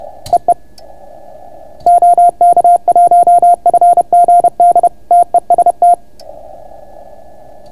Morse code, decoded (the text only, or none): EE OK1FGD TEST